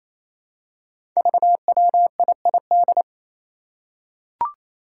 Morse code, heard as a keyed tone, about 28 wpm, 700 Hz. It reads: VWIID
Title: Five Challenging Letter-Number Combinations 28wpm